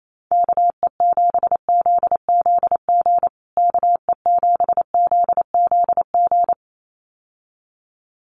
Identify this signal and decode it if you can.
Morse code, heard as a keyed tone, about 28 wpm, 700 Hz. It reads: KE7ZZG KE7ZZG
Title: USA Call Signs Repeated 2x 28wpm